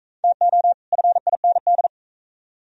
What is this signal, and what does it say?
Morse code, heard as a keyed tone, about 42 wpm, 700 Hz. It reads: TO FIND